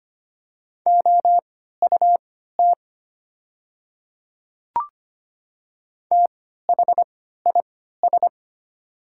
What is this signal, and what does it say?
Morse code, heard as a keyed tone, about 25 wpm, spaced as for 14 wpm, 690 Hz. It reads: OUT THIS
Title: 3x Character Spacing - Top 200 Words - 25wpm